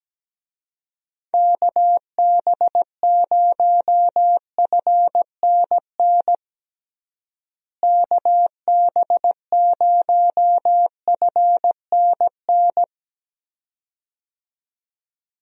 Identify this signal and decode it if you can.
Morse code, heard as a keyed tone, about 17 wpm, 700 Hz. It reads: KB0FNN KB0FNN
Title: USA Call Signs - 6 Characters - Repeated 2x - 17wpm